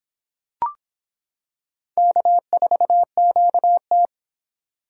Morse code, heard as a keyed tone, about 26 wpm, 700 Hz. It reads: K4QT